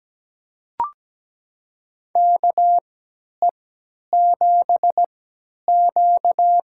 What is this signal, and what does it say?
Morse code, heard as a keyed tone, about 17 wpm, spaced as for 10 wpm, 700 Hz. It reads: KE7Q
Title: USA Call Signs - 3x Character Spacing - 17wpm